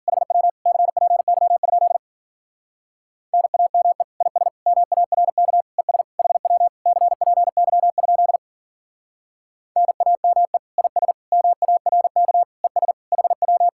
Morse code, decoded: HW CPY? NAME IS MARK ES HW CPY? NAME IS MARK ES HW

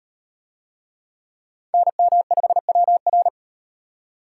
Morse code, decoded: NM5WR

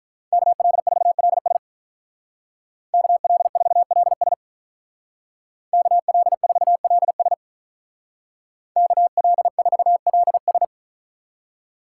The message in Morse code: KL4LS KL4LS KL4LS KL4LS

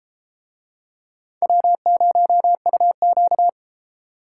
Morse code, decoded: W0UQ

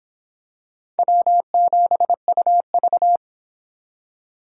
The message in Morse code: W7UV

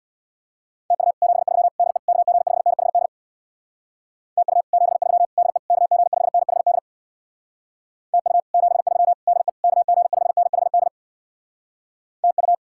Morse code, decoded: TU 73 DE KG5NHD TU 73 DE KG5NHD TU 73 DE KG5NHD TU

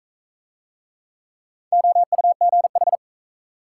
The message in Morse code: OUGH

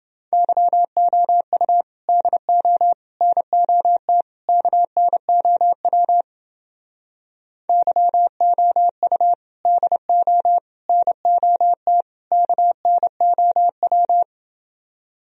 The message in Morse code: YOU DO NOT KNOW YOU DO NOT KNOW

